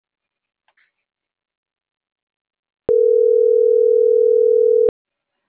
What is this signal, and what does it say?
Ringback tone (North America), 440 + 480 Hz.